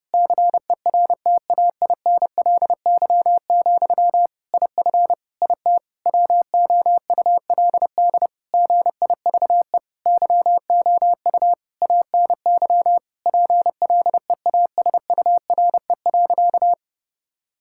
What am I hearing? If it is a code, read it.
Morse code, heard as a keyed tone, about 29 wpm, 700 Hz. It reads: CERTAINLY, IF IT WOULD GIVE YOU ANY PLEASURE.
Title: Sentences from Top 1600 Words 30wpm